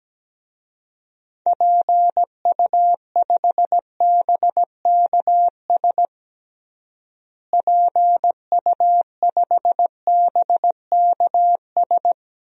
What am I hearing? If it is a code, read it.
Morse code, heard as a keyed tone, about 17 wpm, 700 Hz. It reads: PU5BKS PU5BKS
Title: DXCC-100 Call Signs - 6 Characters - Repeated 3x - 17wpm